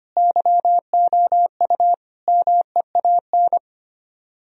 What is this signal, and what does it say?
Morse code, heard as a keyed tone, about 31 wpm, 700 Hz. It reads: YOU MEAN
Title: Speed-Racing - Sets of 4 Words 25wpm